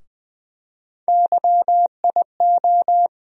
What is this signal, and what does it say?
Morse code, heard as a keyed tone, about 20 wpm, 700 Hz. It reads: YIO
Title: USA Call Sign Postfixes - 3 Letters 20wpm